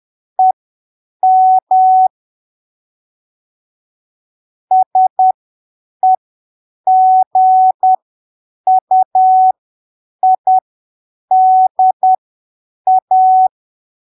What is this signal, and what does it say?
Morse code, heard as a keyed tone, about 10 wpm, spaced as for 7 wpm, 750 Hz. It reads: EM SEGUIDA
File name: arquivos / texto / texto 08 10ppm